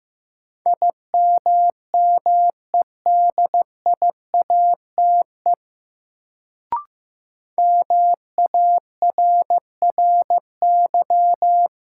Morse code, heard as a keyed tone, about 15 wpm, 700 Hz. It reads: IMMEDIATE MARRY
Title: Sentences from Top 1200 Words - New Word Drill Alt - 15wpm